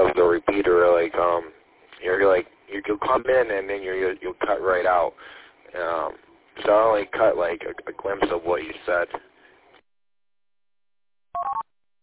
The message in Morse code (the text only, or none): none